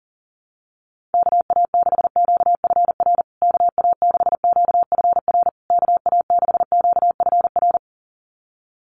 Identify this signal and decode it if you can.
Morse code, heard as a keyed tone, about 40 wpm, 700 Hz. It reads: KA6QFR KA6QFR KA6QFR
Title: USA Call Signs Repeated 3x 40wpm